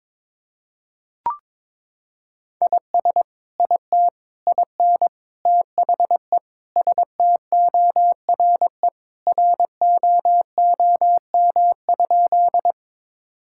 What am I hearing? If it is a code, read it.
Morse code, heard as a keyed tone, about 22 wpm, 700 Hz. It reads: IS IT IN THE STORE ROOM?
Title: Sentences from Top 1000 Words - Review Part 5 - 22wpm